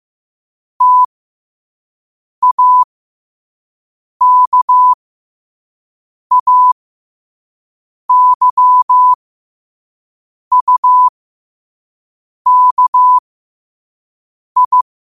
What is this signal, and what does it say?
Morse code, heard as a keyed tone, about 15 wpm, spaced as for 5 wpm, 1000 Hz. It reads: TAKAYUKI
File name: qotdfiles / 5wpm